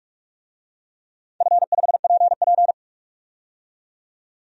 Morse code, heard as a keyed tone, about 46 wpm, 700 Hz. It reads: F5PP
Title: DXCC-100 Call Signs - 4 Characters - 45wpm